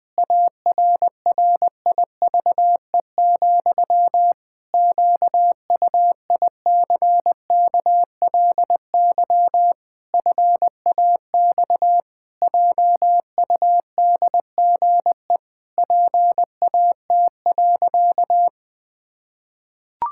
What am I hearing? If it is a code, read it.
Morse code, heard as a keyed tone, about 20 wpm, 700 Hz. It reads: ARRIVE, QUICKLY FAX JUDGE PAT.
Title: Pangram Challenge v2 - 20wpm